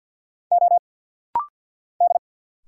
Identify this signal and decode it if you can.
Morse code, heard as a keyed tone, about 50 wpm, 690 Hz. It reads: O D
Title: Single Letters - Rapid-Fire 50wpm